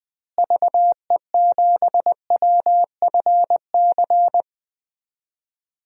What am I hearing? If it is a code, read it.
Morse code, heard as a keyed tone, about 20 wpm, 700 Hz. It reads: VE7WFC
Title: DXCC-100 Call Signs - 6 Characters - Repeated 3x - 20wpm